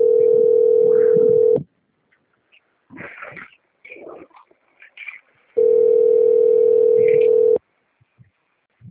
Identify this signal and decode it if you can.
Ringback tone (North America), 440 + 480 Hz.